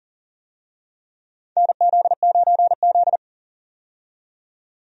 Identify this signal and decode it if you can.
Morse code, heard as a keyed tone, about 38 wpm, 690 Hz. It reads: NZ9Z
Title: USA Call Signs - 4 Characters - 40wpm